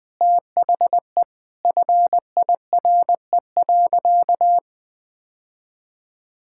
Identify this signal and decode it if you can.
Morse code, heard as a keyed tone, about 20 wpm, 700 Hz. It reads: THE FIRE.
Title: Sentences from Top 1600 Words 20wpm